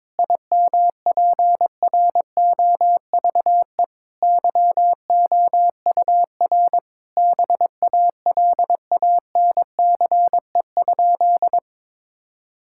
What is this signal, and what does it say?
Morse code, heard as a keyed tone, about 22 wpm, 700 Hz. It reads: IMPROVE YOUR BALANCE?
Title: Sentences from Top 1000 Words - Review Part 6 - 22wpm